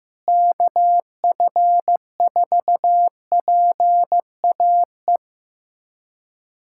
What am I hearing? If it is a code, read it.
Morse code, heard as a keyed tone, about 16 wpm, 700 Hz. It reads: KF4PAE
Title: USA Callsign DE Callsign 15wpm